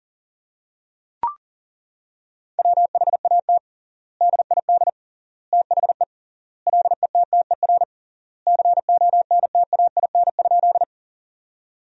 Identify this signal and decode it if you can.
Morse code, heard as a keyed tone, about 40 wpm, 700 Hz. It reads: WHAT DID THE LETTER CONTAIN?
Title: 3x Word-Spacing - Sentences from Top 500 Words - 40wpm